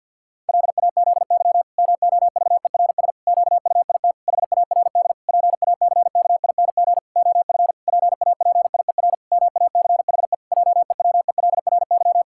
Morse code, heard as a keyed tone, about 50 wpm, 700 Hz. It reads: LAZY MOVERS QUIT HARD PACKING OF PAPIER MACHE JEWELRY
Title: Pangram Challenge v2 - 50wpm